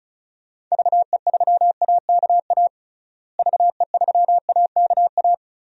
Morse code, heard as a keyed tone, about 36 wpm, 700 Hz. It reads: VE3AKA VE3AKA
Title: DXCC-100 Call Signs - 6 Characters - Repeated 3x - 35wpm